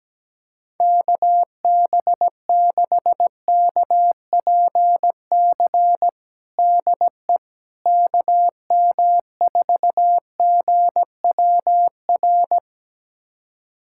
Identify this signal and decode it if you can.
Morse code, heard as a keyed tone, about 17 wpm, 700 Hz. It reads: KB6KPC DE KM4GWR
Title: USA Callsign DE Callsign - 6 Characters - 17wpm